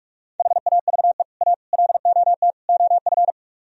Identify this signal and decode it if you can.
Morse code, heard as a keyed tone, about 48 wpm, 700 Hz. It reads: HAVE A LOT OF